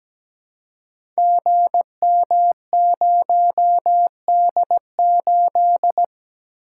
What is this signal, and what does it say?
Morse code, heard as a keyed tone, about 17 wpm, 700 Hz. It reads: GM0D8
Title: Code Groups - 5 Letter:Numbers - 17wpm